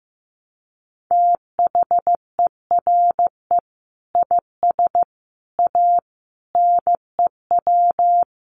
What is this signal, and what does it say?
Morse code, heard as a keyed tone, about 15 wpm, 700 Hz. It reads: THERE IS A NEW
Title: Sentences from Top 1500 Words 15wpm